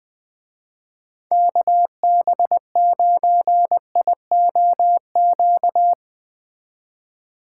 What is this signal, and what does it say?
Morse code, heard as a keyed tone, about 20 wpm, 700 Hz. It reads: KB9IOQ
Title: Speed Racing - USA 2x3 Call Signs 17wpm